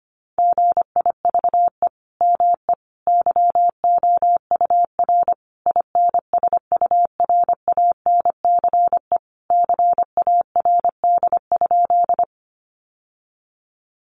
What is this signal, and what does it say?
Morse code, heard as a keyed tone, about 25 wpm, 700 Hz. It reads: GIVE ME YOUR INSURANCE CARD?
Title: Sentences from Top 1100 Words 25wpm